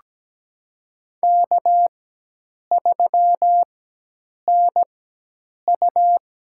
Morse code, heard as a keyed tone, about 17 wpm, spaced as for 8 wpm, 700 Hz. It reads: K3NU
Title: USA Call Signs - 4 Characters - 4x Character Spacing - 17wpm